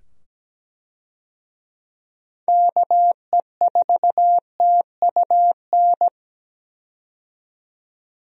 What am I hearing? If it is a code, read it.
Morse code, heard as a keyed tone, about 17 wpm, 700 Hz. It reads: KE4TUN
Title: USA Call Signs - 6 Characters - 17wpm